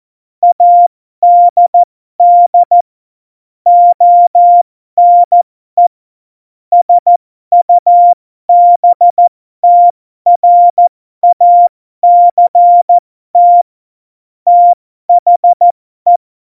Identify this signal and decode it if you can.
Morse code, heard as a keyed tone, about 14 wpm, spaced as for 12 wpm, 700 Hz. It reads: ADD ONE SUBTRACT THE
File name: Morsekurs / Witze